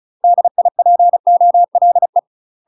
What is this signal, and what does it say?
Morse code, heard as a keyed tone, about 35 wpm, 700 Hz. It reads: DIPOLE
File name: UK MASTER QSOs TRAINING / 35WPM / UK MASTER QSOs 0037